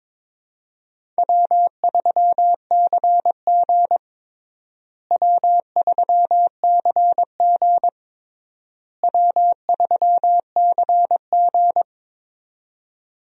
Morse code, heard as a keyed tone, about 22 wpm, 700 Hz. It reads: W3CG W3CG W3CG
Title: USA Call Signs - 4 Characters - Repeated 3x - 22wpm